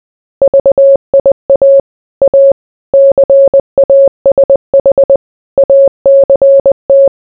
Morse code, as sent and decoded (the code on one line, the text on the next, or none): ...- .. .- / .- / -.-. .- ... .... / .- -.-. -
VIA A CASH ACT